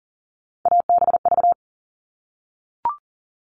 Morse code, heard as a keyed tone, about 40 wpm, 700 Hz. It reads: ABV
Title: All Cut Numbers - 3-Digit 40wpm